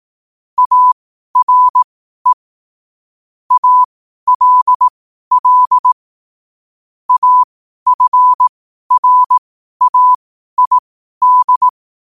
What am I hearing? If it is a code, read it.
Morse code, heard as a keyed tone, about 18 wpm, spaced as for 12 wpm, 1000 Hz. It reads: ARE ALL AFRAID
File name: qotdfiles / 12wpm